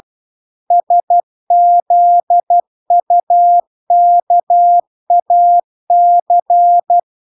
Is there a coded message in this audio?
Yes: Morse code from 0.7 to 7.0 s, heard as a keyed tone, about 12 wpm, 700 Hz. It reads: SZUKAC